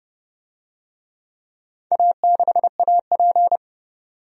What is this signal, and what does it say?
Morse code, heard as a keyed tone, about 30 wpm, 700 Hz. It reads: A6AP
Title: DXCC-100 Call Signs - 4 Characters - 30wpm